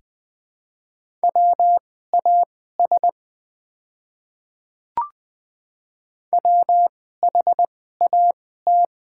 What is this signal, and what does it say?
Morse code, heard as a keyed tone, about 20 wpm, spaced as for 14 wpm, 700 Hz. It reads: WAS WHAT
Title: 2x Character Spacing - Top 100 Words - 20wpm